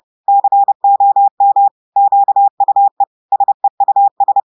Morse code, heard as a keyed tone, about 30 wpm, 800 Hz. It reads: COM QUE SEUS
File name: portugais texte en clair 2 vitesse 30 mots minutes